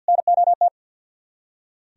Morse code, heard as a keyed tone, about 64 wpm, 700 Hz. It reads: NOT